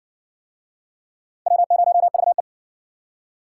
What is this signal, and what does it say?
Morse code, heard as a keyed tone, about 60 wpm, 700 Hz. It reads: W0VE